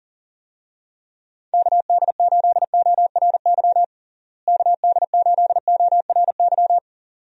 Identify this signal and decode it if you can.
Morse code, heard as a keyed tone, about 40 wpm, 700 Hz. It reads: KD8ORY KD8ORY
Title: USA Call Signs - 6 Characters - Repeated 2x - 40wpm